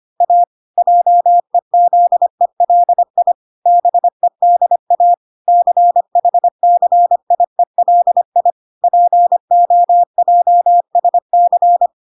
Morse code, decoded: A JEZELI BEDA CHCIELI POJSC

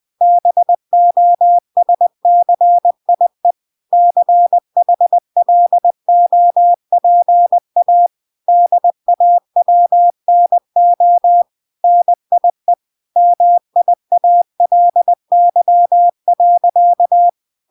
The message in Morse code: BOSCIE CHLOPA DAWNO NIE MIALY.

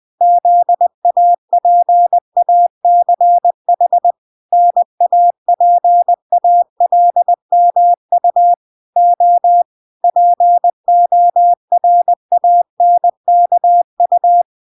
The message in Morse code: ZAPACH NAPALMU O PORANKU

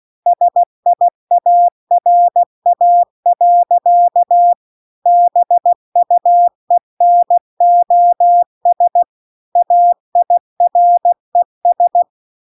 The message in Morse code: SIARA. BUENOS AIRES

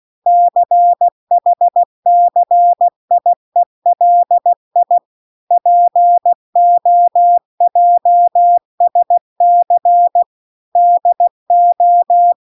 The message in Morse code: CHCIELI POJSC DO

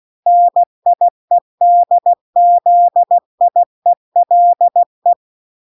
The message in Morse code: NIEDZIELE

